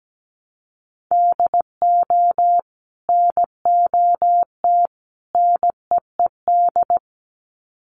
Morse code, heard as a keyed tone, about 17 wpm, 700 Hz. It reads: DO NOT NEED
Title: Sets of 3 Words 17wpm